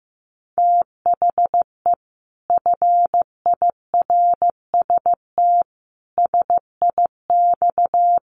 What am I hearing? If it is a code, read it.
Morse code, heard as a keyed tone, about 15 wpm, 700 Hz. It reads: THE FIRST SIX
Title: Sets of 5 Words Encore2 15wpm